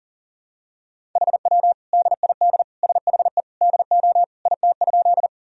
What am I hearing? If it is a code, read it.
Morse code, heard as a keyed tone, about 40 wpm, 700 Hz. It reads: HW DID SHE DO IT?
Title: Sentences from Top 100 Words and Abbreviations 40wpm